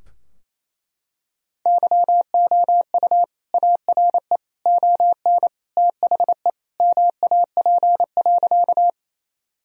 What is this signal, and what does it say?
Morse code, heard as a keyed tone, about 28 wpm, 700 Hz. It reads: YOU ARE ON THE MAP.
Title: Sentences from Top 1000 Words - Review Part 2 - 28wpm